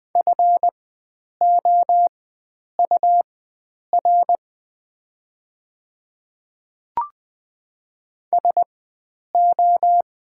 Morse code, heard as a keyed tone, about 20 wpm, spaced as for 9 wpm, 700 Hz. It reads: FOUR SO
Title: 4x Character Spacing - Top 500 Words - 20wpm